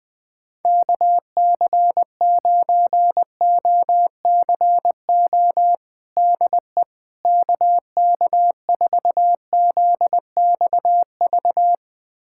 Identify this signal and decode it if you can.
Morse code, heard as a keyed tone, about 20 wpm, 700 Hz. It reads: KC9OCO DE KK4ZXV
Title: USA Callsign DE Callsign 20wpm